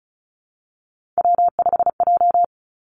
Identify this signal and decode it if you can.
Morse code, heard as a keyed tone, about 36 wpm, 700 Hz. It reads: W5J
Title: Three Challenging Letter-Number Combinations 35wpm